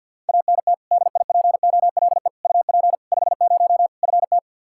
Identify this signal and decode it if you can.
Morse code, heard as a keyed tone, about 52 wpm, 700 Hz. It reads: ANT DIPOLE UP 50 FT
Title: K1USN SST QSO Practice 50wpm